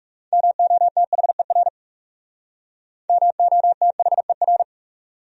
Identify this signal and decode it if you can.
Morse code, heard as a keyed tone, about 36 wpm, 700 Hz. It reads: MOTHER MOTHER
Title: Speed-Racing Top 500 Words 30wpm